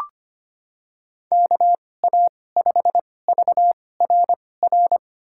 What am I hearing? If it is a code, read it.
Morse code, heard as a keyed tone, about 25 wpm, spaced as for 18 wpm, 700 Hz. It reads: KA5VRR
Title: USA Call Signs - 6 Characters - 2x Character Spacing - 25wpm